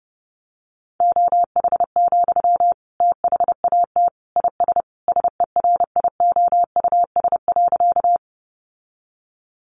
Morse code, heard as a keyed tone, about 30 wpm, 700 Hz. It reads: OH, THAT IS SERIOUS.
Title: Sentences from Top 1100 Words 30wpm